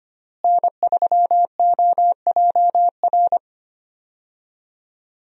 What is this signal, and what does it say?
Morse code, heard as a keyed tone, about 25 wpm, 700 Hz. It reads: N3OJR
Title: USA Call Signs - 5 Characters - Repeated 2x - 25wpm